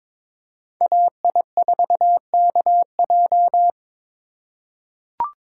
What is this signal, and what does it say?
Morse code, heard as a keyed tone, about 22 wpm, 700 Hz. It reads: AI4KJ
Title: USA Call Signs - 5 Characters - 3x Character Spacing - 22wpm